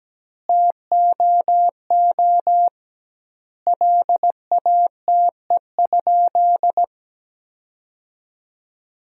Morse code, heard as a keyed tone, about 17 wpm, 700 Hz. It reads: TOO LATE?
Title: 2x Word-Spacing - Sentences from Top 300 Words - 17wpm